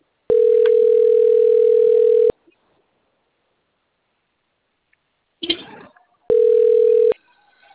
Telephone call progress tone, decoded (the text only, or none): ringback tone